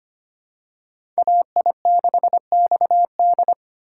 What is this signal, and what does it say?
Morse code, heard as a keyed tone, about 25 wpm, 700 Hz. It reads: AI6XD
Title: USA Call Signs - 5 Characters - 4x Character Spacing - 25wpm